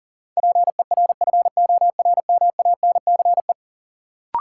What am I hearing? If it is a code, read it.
Morse code, heard as a keyed tone, about 40 wpm, 700 Hz. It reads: PERFORMANCE
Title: Sentences from Top 1000 Words - New Word Drill - 40wpm